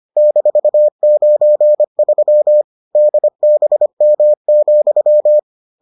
Morse code, heard as a keyed tone, about 25 wpm, 600 Hz. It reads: -93 DBM,